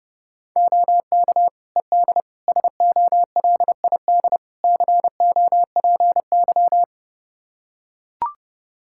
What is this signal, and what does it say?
Morse code, heard as a keyed tone, about 30 wpm, 700 Hz. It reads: OK ED SOLID COPY
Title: Practice QSO - Exchange 3 - More Info - 30wpm